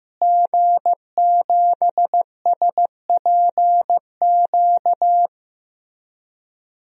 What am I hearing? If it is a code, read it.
Morse code, heard as a keyed tone, about 15 wpm, 700 Hz. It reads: G7SPQ
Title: DXCC-100 Call Signs - 5 Characters - Repeated 2x - 15wpm